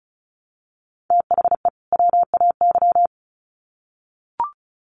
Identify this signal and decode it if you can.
Morse code, heard as a keyed tone, about 34 wpm, 700 Hz. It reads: THE WAY
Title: Sets of 2 Words 35wpm